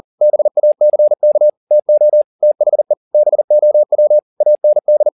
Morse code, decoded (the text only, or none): BACK TO THE BOW AND